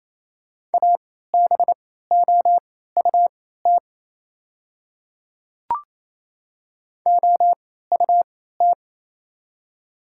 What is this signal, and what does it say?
Morse code, heard as a keyed tone, about 27 wpm, spaced as for 16 wpm, 700 Hz. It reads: ABOUT OUT
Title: 3x Character Spacing - Top 100 Words - 28wpm